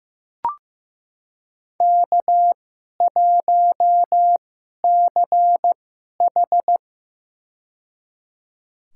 Morse code, heard as a keyed tone, about 15 wpm, spaced as for 11 wpm, 700 Hz. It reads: K1CH
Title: USA Call Signs - 2x Character Spacing - 15wpm